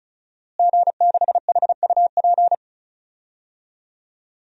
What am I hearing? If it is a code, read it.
Morse code, heard as a keyed tone, about 36 wpm, 700 Hz. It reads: G6HUP
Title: DXCC-100 Call Signs - 5 Characters - Repeated 2x - 35wpm